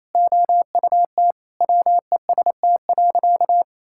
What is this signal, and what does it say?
Morse code, heard as a keyed tone, about 28 wpm, 700 Hz. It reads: OUT WEST.
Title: Sentences from Top 1000 Words - Review Part 5 - 28wpm